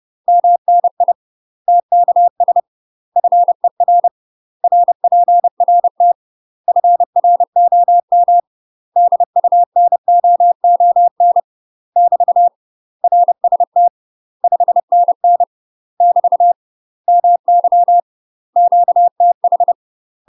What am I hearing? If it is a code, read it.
Morse code, heard as a keyed tone, about 30 wpm, 700 Hz. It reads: MNI TKS FER RPRT FROM DUNOON <BT> RST 5NN <BT> MY QTH
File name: UK MASTER QSOs TRAINING / 30WPM / UK MASTER QSOs 0030